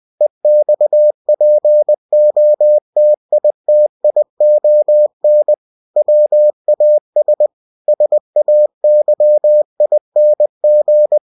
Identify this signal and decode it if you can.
Morse code, heard as a keyed tone, about 20 wpm, 600 Hz. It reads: EXPOTITION WAS SAYING